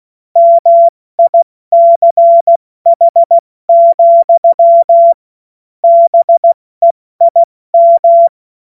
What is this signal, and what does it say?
Morse code, heard as a keyed tone, about 16 wpm, 680 Hz. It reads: MICH, BEIM
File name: Witz-22-Juli-2025-70-F80